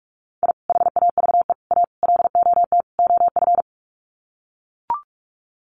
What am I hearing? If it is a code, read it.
Morse code, heard as a keyed tone, about 44 wpm, 700 Hz. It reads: I HAVE A LOT OF
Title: Sets of 5 Words 45wpm